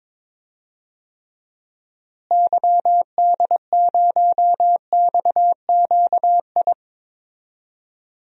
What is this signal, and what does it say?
Morse code, heard as a keyed tone, about 22 wpm, 700 Hz. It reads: YD0XQI
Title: DXCC-100 Call Signs - 6 Characters - Repeated 2x - 22wpm